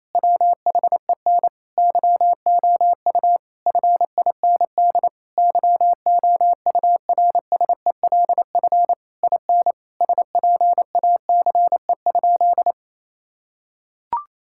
Morse code, heard as a keyed tone, about 27 wpm, 700 Hz. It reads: WHEN YOU FIND YOURSELF IN SPACE?
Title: Sentences from Top 1000 Words - Review Part 5 - 28wpm